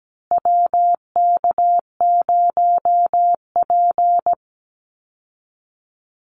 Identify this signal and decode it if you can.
Morse code, heard as a keyed tone, about 17 wpm, 700 Hz. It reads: WK0P
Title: Call Signs Repeated 2x - 17wpm